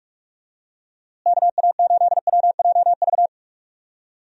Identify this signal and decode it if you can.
Morse code, heard as a keyed tone, about 45 wpm, 700 Hz. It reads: KA8WJV
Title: USA Call Signs - 6 Characters - Repeated 2x - 45wpm